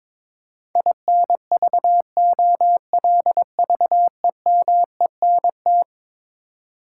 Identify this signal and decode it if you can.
Morse code, heard as a keyed tone, about 22 wpm, 700 Hz. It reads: INVOLVEMENT
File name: practice / 22wpm / Eleven Letter Words - 22wpm